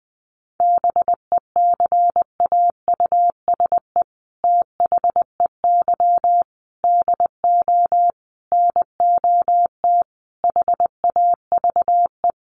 Morse code, heard as a keyed tone, about 20 wpm, 700 Hz. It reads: BECAUSE THEY DO NOT HAVE
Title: Sets of 5 Words Encore2 20wpm